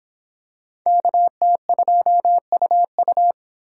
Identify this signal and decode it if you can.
Morse code, heard as a keyed tone, about 24 wpm, 700 Hz. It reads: KT2UU